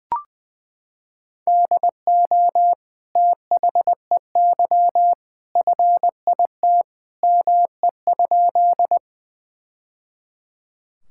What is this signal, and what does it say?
Morse code, heard as a keyed tone, about 21 wpm, 700 Hz. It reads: DO THEY FIT ME?
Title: Sentences from Top 1000 Words - Review Part 5 - 20wpm